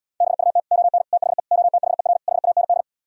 Morse code, heard as a keyed tone, about 77 wpm, 700 Hz. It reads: BUT ON THE OTHER HAND